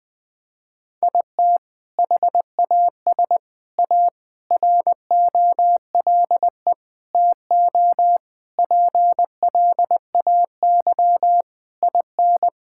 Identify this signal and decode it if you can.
Morse code, heard as a keyed tone, about 20 wpm, 700 Hz. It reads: IT HAS A ROLE TO PLAY IN
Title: Sentences from Top 1000 Words - Review Part 6 - 20wpm